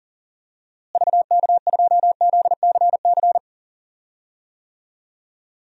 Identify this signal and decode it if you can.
Morse code, heard as a keyed tone, about 37 wpm, 700 Hz. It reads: VK2ZCC
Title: DXCC-100 Call Signs - 6 Characters - Repeated 2x - 40wpm